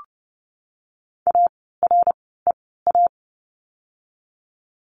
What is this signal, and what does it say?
Morse code, heard as a keyed tone, about 30 wpm, spaced as for 17 wpm, 700 Hz. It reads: AREA
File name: Lesson 16f - Letter c - Words 30wpm